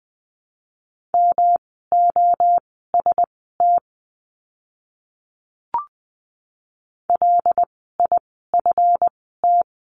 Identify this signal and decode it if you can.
Morse code, heard as a keyed tone, about 20 wpm, spaced as for 14 wpm, 700 Hz. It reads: MOST LIFT
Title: 139 Lesson 23f - Letter f - Words 20wpm